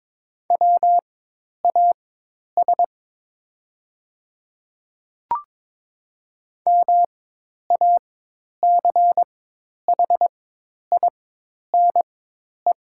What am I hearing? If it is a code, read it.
Morse code, heard as a keyed tone, about 22 wpm, spaced as for 10 wpm, 700 Hz. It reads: WAS MACHINE
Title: 4x Character Spacing - Top 500 Words - 22wpm